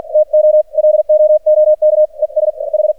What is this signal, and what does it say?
Morse code, heard as a keyed tone, about 44 wpm, 610 Hz. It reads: TO LOOK ES H
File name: archived / CRSnet